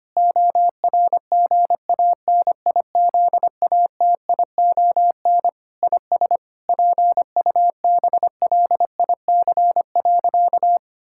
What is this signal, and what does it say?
Morse code, heard as a keyed tone, about 25 wpm, 700 Hz. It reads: ORGANIZATION IS PUBLIC.
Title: Sentences from Top 1000 Words - Review Part 5 - 25wpm